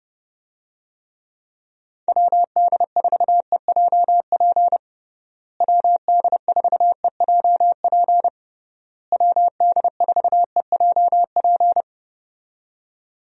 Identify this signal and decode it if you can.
Morse code, heard as a keyed tone, about 32 wpm, 700 Hz. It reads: WD4EJP WD4EJP WD4EJP
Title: USA Call Signs - 6 Characters - Repeated 3x - 30wpm